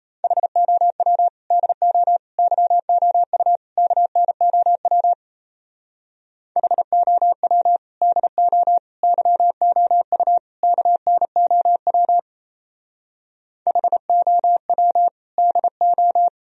Morse code, heard as a keyed tone, about 34 wpm, 700 Hz. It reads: HOW DO YOU KNOW HOW DO YOU KNOW HOW DO
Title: Speed-Racing - Sets of 4 Words 28wpm